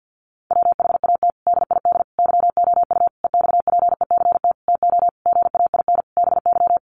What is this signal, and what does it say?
Morse code, heard as a keyed tone, about 50 wpm, 700 Hz. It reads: WHAT DID YOU EXPECT TO GAIN BY
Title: Sentences from Top 1100 Words 50wpm